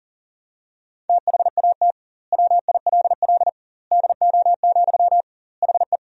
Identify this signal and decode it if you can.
Morse code, heard as a keyed tone, about 40 wpm, 700 Hz. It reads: THAT WILL DO, HE
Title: 2x Word-Spacing - Sentences from Top 100 Words - 40wpm